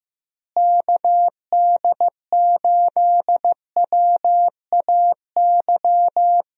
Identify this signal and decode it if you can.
Morse code, heard as a keyed tone, about 15 wpm, 700 Hz. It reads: KD8WAY
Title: USA Call Signs - 6 Characters - Repeated 2x - 15wpm